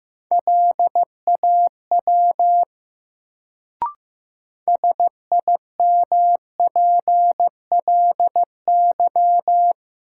Morse code, heard as a keyed tone, about 15 wpm, 700 Hz. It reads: LAW SIMPLY
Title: Sentences from Top 600 Words - New Word Drill - 15wpm